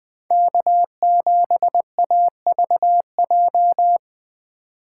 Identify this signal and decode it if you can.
Morse code, heard as a keyed tone, about 21 wpm, 700 Hz. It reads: K7AVJ
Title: Speed Racing - USA 1x3 Call Signs 17wpm